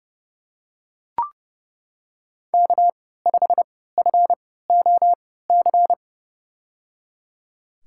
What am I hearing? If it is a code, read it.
Morse code, heard as a keyed tone, about 30 wpm, spaced as for 17 wpm, 700 Hz. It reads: K5FOC
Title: USA Call Signs - 5 Characters - 3x Character Spacing - 30wpm